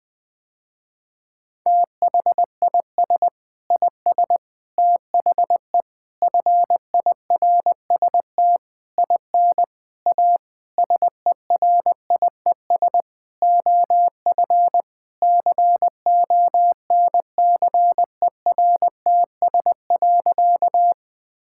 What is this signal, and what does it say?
Morse code, heard as a keyed tone, about 20 wpm, 700 Hz. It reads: THIS IS THE FIRST IN A SERIES OF CONCERTS.
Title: Sentences with 8-Word Phrases - Spoken Beginning - 20wpm